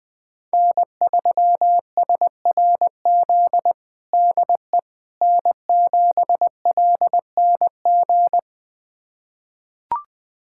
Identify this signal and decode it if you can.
Morse code, heard as a keyed tone, about 21 wpm, 700 Hz. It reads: N3SRZ DE N7LNG
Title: USA Callsign DE Callsign - 5 Characters - 20wpm